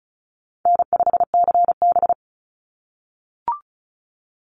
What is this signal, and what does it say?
Morse code, heard as a keyed tone, about 34 wpm, 700 Hz. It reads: N5CB
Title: USA Call Signs 35wpm